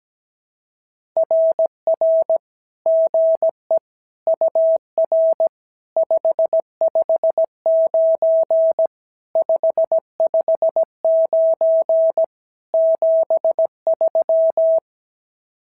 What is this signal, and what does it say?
Morse code, heard as a keyed tone, about 16 wpm, 650 Hz. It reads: RR GE UR 559 559 73
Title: SOTA QSO Practice 17wpm